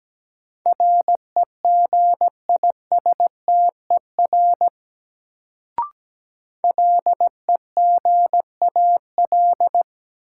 Morse code, heard as a keyed tone, about 17 wpm, 700 Hz. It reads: REGISTER LEGAL
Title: Sentences from Top 1200 Words - New Word Drill Alt - 17wpm